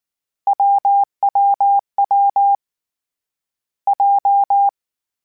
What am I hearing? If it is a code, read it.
Morse code, heard as a keyed tone, about 19 wpm, 800 Hz. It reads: WWW J
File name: Difficult Combos / Difficult Combos-mp3s / AWJ @18